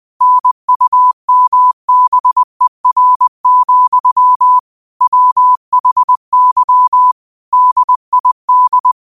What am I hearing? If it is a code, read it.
Morse code, heard as a keyed tone, about 20 wpm, 1000 Hz. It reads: NUMBER, WHY DID
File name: qotdfiles / 20wpm